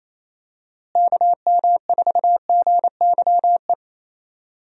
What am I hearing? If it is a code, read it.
Morse code, heard as a keyed tone, about 27 wpm, 700 Hz. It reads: KM4GYE
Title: USA Call Signs - 6 Characters - Repeated 3x - 28wpm